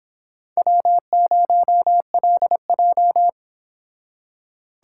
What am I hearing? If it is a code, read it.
Morse code, heard as a keyed tone, about 26 wpm, 700 Hz. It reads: W0LJ